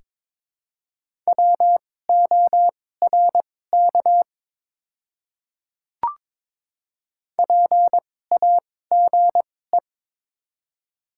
Morse code, heard as a keyed tone, about 22 wpm, spaced as for 16 wpm, 700 Hz. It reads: WORK PAGE
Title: 2x Character Spacing - Top 300 Words - 22wpm